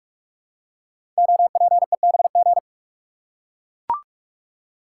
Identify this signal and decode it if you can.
Morse code, heard as a keyed tone, about 44 wpm, 700 Hz. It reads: OPEBG
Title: Code Groups - 5 Letter:Numbers - 45wpm